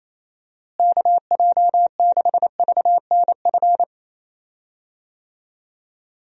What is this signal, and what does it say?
Morse code, heard as a keyed tone, about 28 wpm, 700 Hz. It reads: KJ6VNF
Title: USA Call Signs - 6 Characters - Repeated 3x - 28wpm